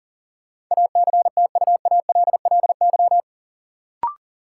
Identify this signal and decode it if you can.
Morse code, heard as a keyed tone, about 40 wpm, 700 Hz. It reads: ACTUALLY
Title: Sentences from Top 700 Words - New Word Drill - 40wpm